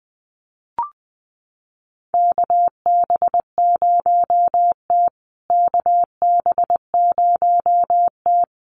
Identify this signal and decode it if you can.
Morse code, heard as a keyed tone, about 20 wpm, 700 Hz. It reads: KB0T KB0T
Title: USA Call Signs Repeated 3x 20wpm